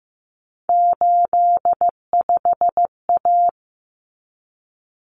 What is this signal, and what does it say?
Morse code, heard as a keyed tone, about 15 wpm, 700 Hz. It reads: 85A
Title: Common Cut Numbers - 3-Digit 15wpm